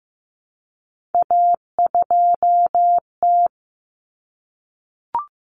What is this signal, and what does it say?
Morse code, heard as a keyed tone, about 15 wpm, 700 Hz. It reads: A2T
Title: Common Cut Numbers - 3-Digit 15wpm